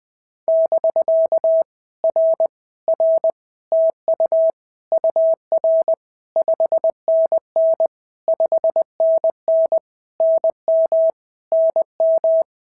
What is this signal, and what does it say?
Morse code, heard as a keyed tone, about 19 wpm, 650 Hz. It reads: <BK> R R TU UR 5NN 5NN NM NM
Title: POTA QSO Practice - 20wpm